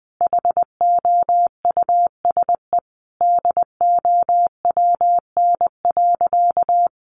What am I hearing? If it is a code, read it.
Morse code, heard as a keyed tone, about 20 wpm, 700 Hz. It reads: HOUSE DOWN.
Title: Sentences from Top 1300 Words 20wpm